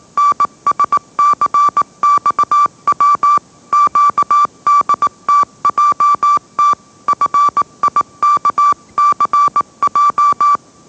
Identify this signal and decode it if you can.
Morse code, heard as a keyed tone, about 18 wpm, 1160 Hz. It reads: NSCXW QDTJT FIKCJ
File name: mrx / mrx081 20wpm